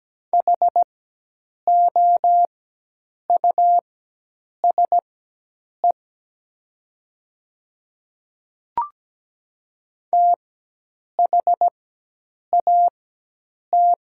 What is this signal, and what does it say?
Morse code, heard as a keyed tone, about 17 wpm, spaced as for 8 wpm, 700 Hz. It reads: HOUSE THAT
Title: 4x Character Spacing - Top 500 Words - 17wpm